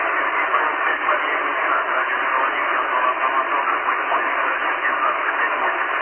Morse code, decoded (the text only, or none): none